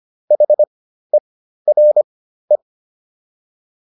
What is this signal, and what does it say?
Morse code, heard as a keyed tone, about 25 wpm, spaced as for 13 wpm, 600 Hz. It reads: HERE